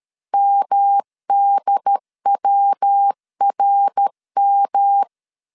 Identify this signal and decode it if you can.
Morse code, heard as a keyed tone, about 13 wpm, 800 Hz. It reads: MDWRM
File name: CW / DET / 1 3 1500C S B 12wpm 2828